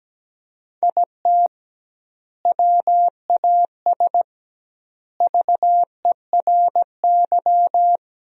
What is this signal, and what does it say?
Morse code, heard as a keyed tone, about 17 wpm, 700 Hz. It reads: IT WAS VERY
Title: 2x Word-Spacing - Sentences from Top 500 Words - 17wpm